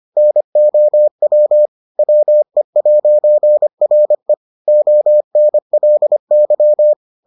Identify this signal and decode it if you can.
Morse code, heard as a keyed tone, about 25 wpm, 600 Hz. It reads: NOW WE'RE ONLY